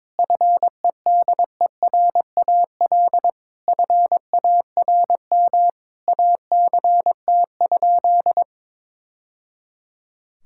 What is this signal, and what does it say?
Morse code, heard as a keyed tone, about 22 wpm, 700 Hz. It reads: FEDERAL FARM ACT?
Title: Sentences from Top 1000 Words - Review Part 5 - 22wpm